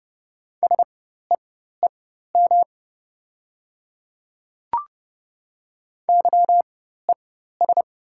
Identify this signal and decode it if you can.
Morse code, heard as a keyed tone, about 30 wpm, spaced as for 14 wpm, 700 Hz. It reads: SEEM YES
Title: 4x Character Spacing - Top 500 Words - 30wpm